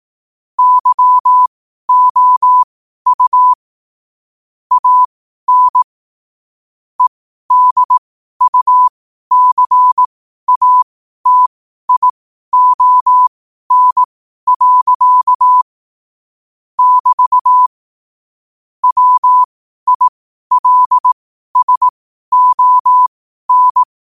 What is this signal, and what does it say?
Morse code, heard as a keyed tone, about 18 wpm, spaced as for 12 wpm, 1000 Hz. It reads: YOU AN EDUCATION. <BT> WILSON
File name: qotdfiles / 12wpm